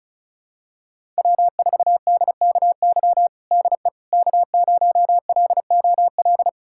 Morse code, heard as a keyed tone, about 36 wpm, 690 Hz. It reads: W4DKY DE K0LOL
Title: USA Callsign DE Callsign - 5 Characters - 35wpm